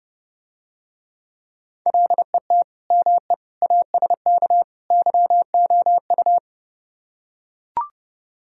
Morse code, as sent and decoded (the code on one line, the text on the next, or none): .-.. - / -- . / .- ... -.- / -.-- --- ..-
LET ME ASK YOU